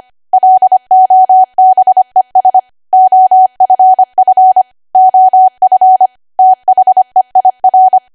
Morse code, heard as a keyed tone, about 25 wpm, 750 Hz. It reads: LOBES OFF OF THEIR
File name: Archive / 25 WPM / 25WPM